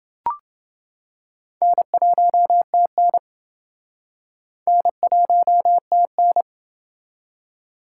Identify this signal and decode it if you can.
Morse code, heard as a keyed tone, about 28 wpm, 700 Hz. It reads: N1TN N1TN